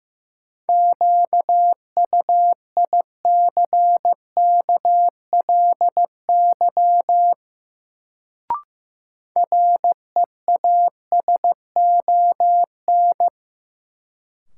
Morse code, heard as a keyed tone, about 15 wpm, 700 Hz. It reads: QUICKLY REASON
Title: Sentences from Top 700 Words - New Word Drill - 15wpm